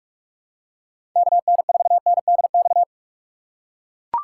Morse code, heard as a keyed tone, about 44 wpm, 700 Hz. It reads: K N 4 N D X
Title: USA Call Signs - 6 Characters - 6x Character Spacing - 45wpm